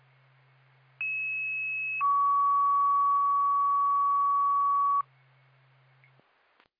Two-tone page, heard as a one-second tone, then a three-second tone, about 2600 then 1130 Hz. It is heard from 1.0 to 5.0 s.